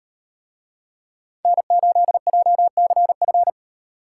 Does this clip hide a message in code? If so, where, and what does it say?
Yes: Morse code from 1.4 to 3.5 s, heard as a keyed tone, about 38 wpm, 700 Hz. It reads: N8JCF